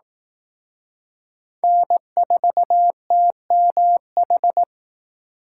Morse code, heard as a keyed tone, about 18 wpm, 700 Hz. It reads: N4TMH
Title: Speed Racing - USA Call Signs 15wpm